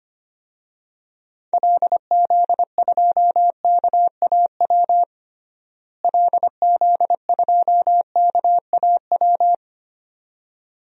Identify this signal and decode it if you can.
Morse code, heard as a keyed tone, about 25 wpm, 700 Hz. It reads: LZ2KAW LZ2KAW
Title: DXCC-100 Call Signs - 6 Characters - Repeated 2x - 25wpm